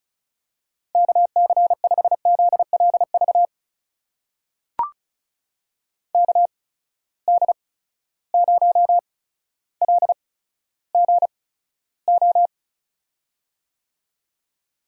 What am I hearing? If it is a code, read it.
Morse code, heard as a keyed tone, about 36 wpm, spaced as for 15 wpm, 700 Hz. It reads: KC5ZLV KD0LGO
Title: USA Call Signs - 6 Characters - 8x Character Spacing - 35wpm